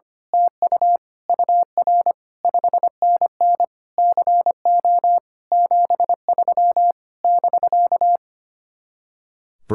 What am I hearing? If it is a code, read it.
Morse code, heard as a keyed tone, about 24 wpm, 700 Hz. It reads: TU UR 5NN CO 73 <BK>
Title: Speed-Racing Common QSO Phrases - 25wpm